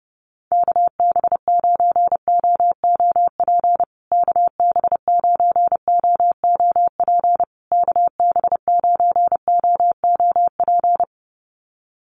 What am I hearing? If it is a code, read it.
Morse code, heard as a keyed tone, about 30 wpm, 700 Hz. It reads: KB9OOP KB9OOP KB9OOP
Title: USA Call Signs Repeated 3x 30wpm